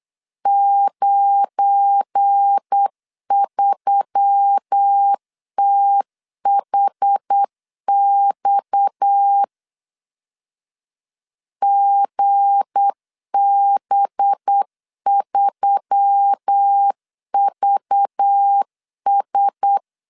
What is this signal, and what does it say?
Morse code, heard as a keyed tone, about 8 wpm, 800 Hz. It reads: 93THX GB3VS
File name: LEZIONI / 7 GF9Z / 7 GF9 8wpm 3545